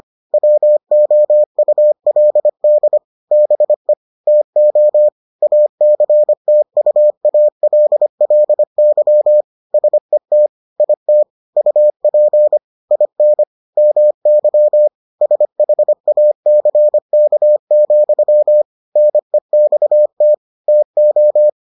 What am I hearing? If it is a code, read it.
Morse code, heard as a keyed tone, about 25 wpm, 600 Hz. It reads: WOULD BE TO ACTUALLY SET IT UP IN MY SHACK, NEXT TO